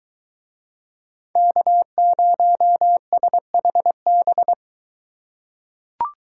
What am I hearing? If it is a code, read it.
Morse code, heard as a keyed tone, about 23 wpm, 700 Hz. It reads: K0SHB